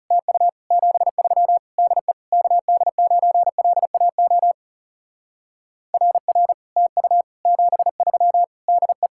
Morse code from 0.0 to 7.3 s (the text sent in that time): TU 73 DE KD9LAO RR TU